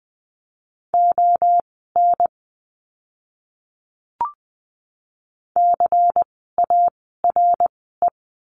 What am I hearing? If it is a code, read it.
Morse code, heard as a keyed tone, about 20 wpm, spaced as for 14 wpm, 700 Hz. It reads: ON CARE
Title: 094 Lesson 16f - Letter c - Words 20wpm